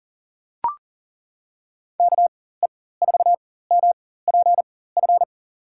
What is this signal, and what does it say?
Morse code, heard as a keyed tone, about 40 wpm, spaced as for 19 wpm, 700 Hz. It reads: KE4MPF
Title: USA Call Signs - 6 Characters - 4x Character Spacing - 40wpm